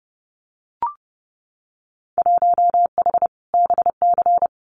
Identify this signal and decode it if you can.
Morse code, heard as a keyed tone, about 31 wpm, 700 Hz. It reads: 1H BC
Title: Winter Field Day Exchanges 30wpm